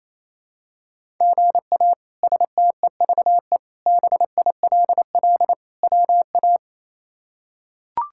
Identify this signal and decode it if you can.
Morse code, heard as a keyed tone, about 27 wpm, 700 Hz. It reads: GA STEVE BILL WA
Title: K1USN SST QSO Practice - 28wpm